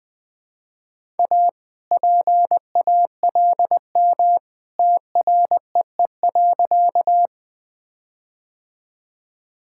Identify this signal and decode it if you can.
Morse code, heard as a keyed tone, about 20 wpm, 700 Hz. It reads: A PALM TREE.
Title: All-Age Appropriate Jokes - 20wpm